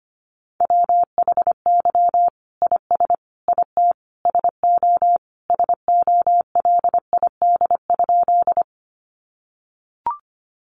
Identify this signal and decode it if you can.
Morse code, heard as a keyed tone, about 25 wpm, 700 Hz. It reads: WHY IS IT SO SOLID?
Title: Sentences from Top 1500 Words 25wpm